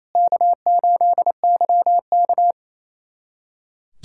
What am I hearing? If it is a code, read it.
Morse code, heard as a keyed tone, about 28 wpm, 690 Hz. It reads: K8YK
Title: USA Callsign DE Callsign - 4 Characters - 28wpm